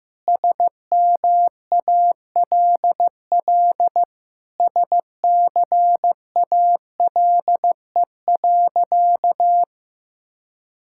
Morse code, heard as a keyed tone, about 16 wpm, 700 Hz. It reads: SMALL SCALE.
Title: Sentences from Top 1000 Words - Review Part 5 - 15wpm